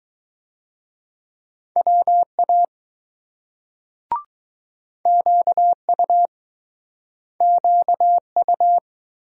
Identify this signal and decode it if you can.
Morse code, heard as a keyed tone, about 20 wpm, 700 Hz. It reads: WA QU QU